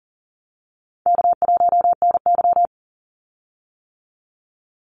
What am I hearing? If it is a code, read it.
Morse code, heard as a keyed tone, about 40 wpm, 700 Hz. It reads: K1NY
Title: Easy Call Signs - 40wpm